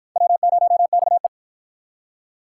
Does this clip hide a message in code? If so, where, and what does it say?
Yes: Morse code from 0.2 to 1.3 s, heard as a keyed tone, about 49 wpm, 700 Hz. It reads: W0XE